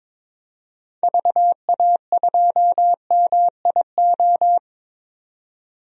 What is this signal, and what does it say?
Morse code, heard as a keyed tone, about 22 wpm, 700 Hz. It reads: VA2MIO
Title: DXCC-100 Call Signs - 6 Characters - 22wpm